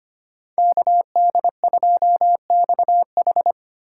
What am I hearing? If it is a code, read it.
Morse code, heard as a keyed tone, about 33 wpm, 700 Hz. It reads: KD2XH